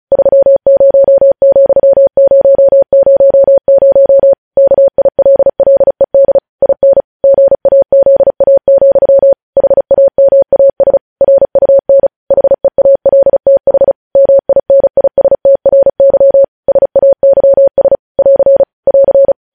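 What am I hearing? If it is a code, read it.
Morse code, heard as a keyed tone, about 36 wpm, 550 Hz. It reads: 30,000 KILLED IN GAZA, HAMAS RUN HEALTH MINISTRY SAYS <AR> <AR>
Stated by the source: Morse code news headlines sent at 35 wpm.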